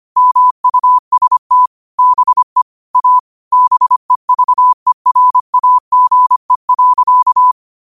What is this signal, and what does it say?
Morse code, heard as a keyed tone, about 25 wpm, 1000 Hz. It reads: MUST BE A BEVERAGE.
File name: qotdfiles / 25wpm